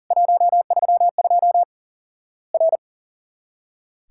Morse code, heard as a keyed tone, about 39 wpm, 700 Hz. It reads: 132 R
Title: Worked All Europe DX Contest - QTC Exchange - 40wpm